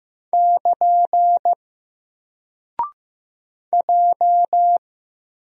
Morse code, heard as a keyed tone, about 15 wpm, 700 Hz. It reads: <KN> J
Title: Single Character (Letter:Number:Punctuation:Prosign) 15wpm